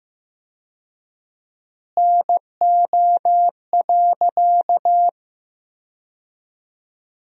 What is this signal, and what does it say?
Morse code, heard as a keyed tone, about 15 wpm, 700 Hz. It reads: NO.
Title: 4x Word-Spacing - Sentences from Top 500 Words - 15wpm